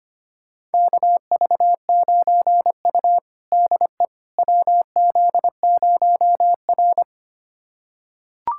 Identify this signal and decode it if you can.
Morse code, heard as a keyed tone, about 25 wpm, 700 Hz. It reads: KV9U DE WZ0R
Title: USA Callsign DE Callsign - 4 Characters - 25wpm